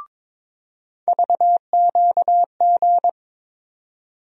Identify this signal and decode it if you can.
Morse code, heard as a keyed tone, about 22 wpm, 700 Hz. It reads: VQG
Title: USA Call Sign Postfixes - 3 Letters 22wpm